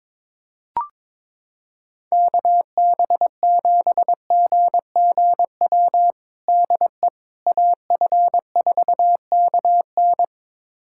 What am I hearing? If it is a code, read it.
Morse code, heard as a keyed tone, about 22 wpm, 700 Hz. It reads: KB7GGW DE AF4KN
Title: USA Callsign DE Callsign 22wpm